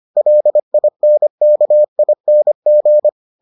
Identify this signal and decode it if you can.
Morse code, heard as a keyed tone, about 25 wpm, 600 Hz. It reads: LINKING